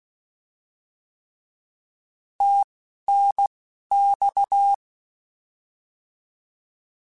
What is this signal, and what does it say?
Morse code, heard as a keyed tone, about 16 wpm, spaced as for 11 wpm, 800 Hz. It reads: TNX
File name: Misc / Misc-mp3s / Words @10